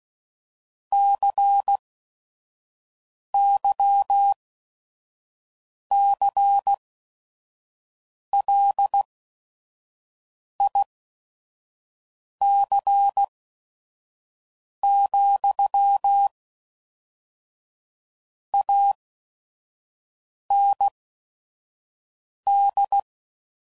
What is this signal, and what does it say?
Morse code, heard as a keyed tone, about 16 wpm, spaced as for 5 wpm, 800 Hz. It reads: CYCLIC, AND
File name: CW Code Groups @5WPM / Technical Subjects / XC